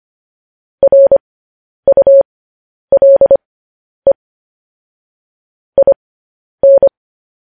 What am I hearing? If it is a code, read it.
Morse code, heard as a keyed tone, about 25 wpm, spaced as for 10 wpm, 550 Hz. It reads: RULE IN
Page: news-headlines-in-morse-code-at-10-wpm